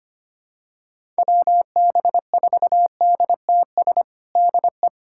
Morse code, heard as a keyed tone, about 25 wpm, 700 Hz. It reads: WB4DTS DE
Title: USA Callsign DE Callsign - 6 Characters - 25wpm